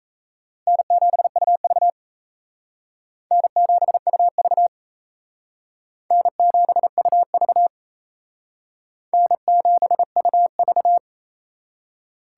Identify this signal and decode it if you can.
Morse code, heard as a keyed tone, about 38 wpm, 700 Hz. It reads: N7UV N7UV N7UV N7UV